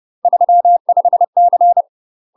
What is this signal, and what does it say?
Morse code, heard as a keyed tone, about 30 wpm, 700 Hz. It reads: 35C
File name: UK MASTER QSOs TRAINING / 30WPM / UK MASTER QSOs 0003